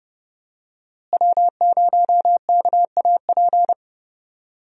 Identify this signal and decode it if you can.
Morse code, heard as a keyed tone, about 32 wpm, 700 Hz. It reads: W0KAP
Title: USA Call Signs - 5 Characters - Repeated 3x - 30wpm